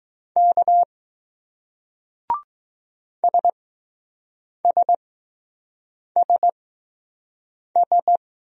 Morse code, heard as a keyed tone, about 20 wpm, spaced as for 6 wpm, 700 Hz. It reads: K SSSS